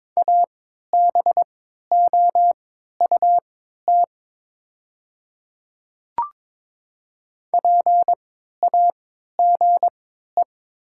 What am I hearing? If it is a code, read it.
Morse code, heard as a keyed tone, about 22 wpm, spaced as for 12 wpm, 700 Hz. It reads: ABOUT PAGE
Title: 3x Character Spacing - Top 300 Words - 22wpm